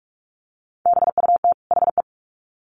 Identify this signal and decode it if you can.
Morse code, heard as a keyed tone, about 44 wpm, 700 Hz. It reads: BUT HE
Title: Sets of 2 Words 45wpm